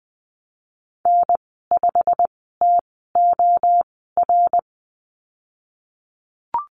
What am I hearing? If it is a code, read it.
Morse code, heard as a keyed tone, about 20 wpm, spaced as for 15 wpm, 700 Hz. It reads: N5TOR
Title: 122 Lesson 20g - Number 3 - Call Signs 20wpm